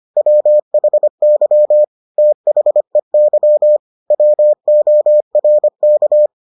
Morse code, heard as a keyed tone, about 25 wpm, 600 Hz. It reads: WHY THEY WORK